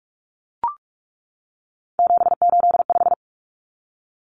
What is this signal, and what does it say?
Morse code, heard as a keyed tone, about 44 wpm, 700 Hz. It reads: 785
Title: Numbers - 3-Digit 45wpm